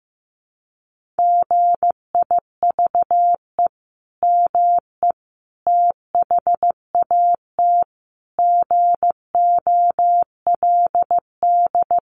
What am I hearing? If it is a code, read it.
Morse code, heard as a keyed tone, about 15 wpm, 700 Hz. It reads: GIVE ME THAT GOLD
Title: Sentences from Top 1500 Words 15wpm